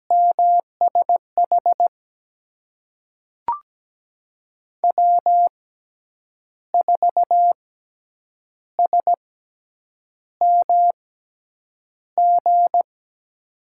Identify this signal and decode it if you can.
Morse code, heard as a keyed tone, about 17 wpm, spaced as for 7 wpm, 700 Hz. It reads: MSH W4SMG
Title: USA Call Signs - 5 Characters - 6x Character Spacing - 17wpm